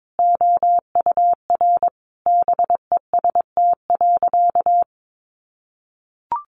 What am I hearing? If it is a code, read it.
Morse code, heard as a keyed tone, about 22 wpm, 700 Hz. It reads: OUR BEST.
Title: Sentences from Top 800 Words 22wpm